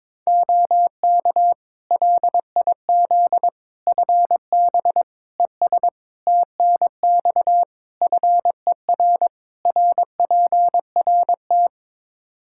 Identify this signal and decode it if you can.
Morse code, heard as a keyed tone, about 22 wpm, 700 Hz. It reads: OK LIZ FB ES TNX FER RPRT
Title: K1USN SST QSO Practice 22wpm